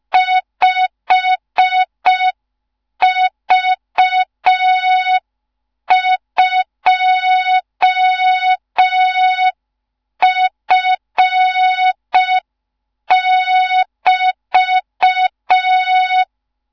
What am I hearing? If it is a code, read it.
Morse code, heard as a keyed tone, about 5 wpm, 3700 Hz. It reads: 5V2F<BT>
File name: Morze-HA2MN / hang / 15 GYAKORLAT